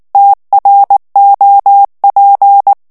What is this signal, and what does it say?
Morse code, heard as a keyed tone, about 19 wpm, 800 Hz. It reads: TROP
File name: FRANCAIS TEXTE CLAIR 2 VITESSE 18 MOTS MINUTES